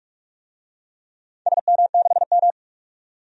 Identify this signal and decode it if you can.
Morse code, heard as a keyed tone, about 47 wpm, 700 Hz. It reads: SM6M
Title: DXCC-100 Call Signs - 4 Characters - 45wpm